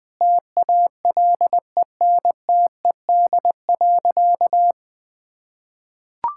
Morse code, heard as a keyed tone, about 20 wpm, 700 Hz. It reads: TALENTED.
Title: Sentences with 5-Word Phrases - Spoken Beginning - 20wpm